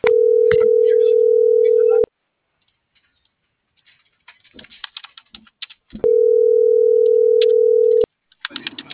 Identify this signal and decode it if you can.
Ringback tone (North America), 440 + 480 Hz.